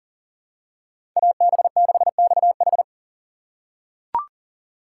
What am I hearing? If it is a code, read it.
Morse code, heard as a keyed tone, about 40 wpm, 700 Hz. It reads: AB6XH
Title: USA Call Signs - 3x Character Spacing - 40wpm